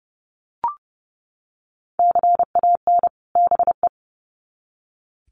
Morse code, heard as a keyed tone, about 30 wpm, 700 Hz. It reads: CAN BE
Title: Sets of 2 Words 30wpm